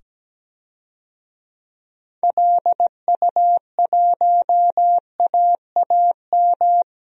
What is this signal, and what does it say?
Morse code, heard as a keyed tone, about 17 wpm, 700 Hz. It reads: LU1AAM
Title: DXCC-100 Call Signs - 6 Characters - 17wpm